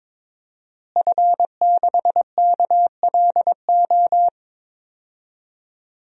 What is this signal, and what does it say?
Morse code, heard as a keyed tone, about 21 wpm, 700 Hz. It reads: F6KLO
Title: DXCC-100 Call Signs - 5 Characters - Repeated 3x - 22wpm